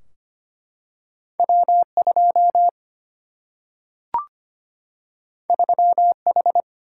Morse code, heard as a keyed tone, about 25 wpm, 700 Hz. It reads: W2 3H
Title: Single-Letter Single-Number 25wpm